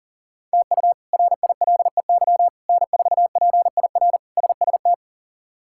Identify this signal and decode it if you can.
Morse code, heard as a keyed tone, about 40 wpm, 700 Hz. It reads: TU RILEY N4PIR SST
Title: K1USN SST QSO Practice - 40wpm